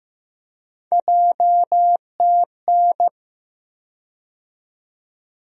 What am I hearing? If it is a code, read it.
Morse code, heard as a keyed tone, about 15 wpm, 700 Hz. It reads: JTN
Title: USA Call Sign Postfixes - 3 Letters 15wpm